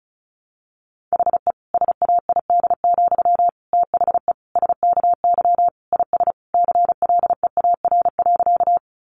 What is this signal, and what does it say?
Morse code, heard as a keyed tone, about 34 wpm, 700 Hz. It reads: HE SAID, THE SKY IS CLEAR.
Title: Sentences from Top 1200 Words 35wpm